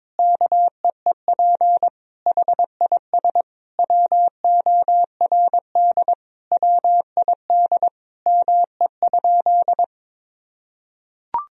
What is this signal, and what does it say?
Morse code, heard as a keyed tone, about 21 wpm, 700 Hz. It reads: KEEP HIS WORD WID ME?
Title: Sentences from Top 300 Words and Abbreviations 22wpm